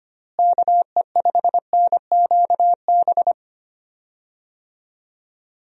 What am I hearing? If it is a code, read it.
Morse code, heard as a keyed tone, about 25 wpm, 700 Hz. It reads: KE5NQB
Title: USA Call Signs - 6 Characters - Repeated 2x - 25wpm